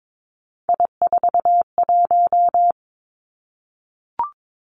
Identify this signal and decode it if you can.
Morse code, heard as a keyed tone, about 22 wpm, 700 Hz. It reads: I41
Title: Three Challenging Letter-Number Combinations 22wpm